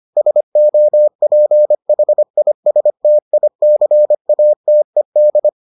Morse code, heard as a keyed tone, about 25 wpm, 600 Hz. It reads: SOPHISTICATED